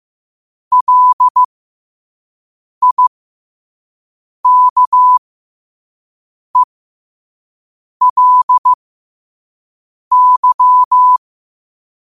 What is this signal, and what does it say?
Morse code, heard as a keyed tone, about 15 wpm, spaced as for 5 wpm, 1000 Hz. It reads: LIKELY
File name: qotdfiles / 5wpm